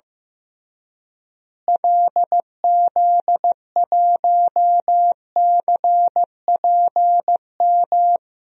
Morse code, heard as a keyed tone, about 15 wpm, 700 Hz. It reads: LZ1CPM
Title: DXCC-100 Call Signs - 6 Characters - Repeated 3x - 15wpm